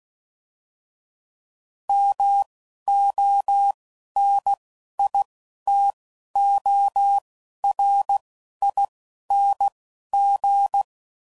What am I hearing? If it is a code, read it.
Morse code, heard as a keyed tone, about 16 wpm, 800 Hz. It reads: M O N I T O R I N G
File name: Suffixes / Special INGs / Single INGs